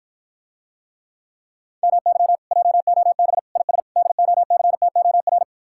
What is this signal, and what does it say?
Morse code, heard as a keyed tone, about 54 wpm, 700 Hz. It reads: MY JOB IS DOCTOR